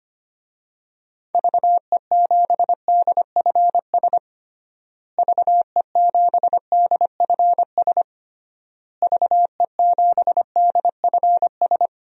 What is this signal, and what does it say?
Morse code, heard as a keyed tone, about 25 wpm, 700 Hz. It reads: VE7DFS VE7DFS VE7DFS
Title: DXCC-100 Call Signs - 6 Characters - Repeated 3x - 25wpm